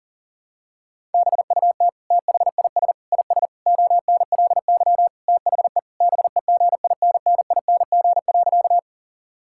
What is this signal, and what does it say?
Morse code, heard as a keyed tone, about 40 wpm, 700 Hz. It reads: BUT THIS IS ONLY THE BEGINNING.
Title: Sentences from Top 1000 Words - Review Part 5 - 40wpm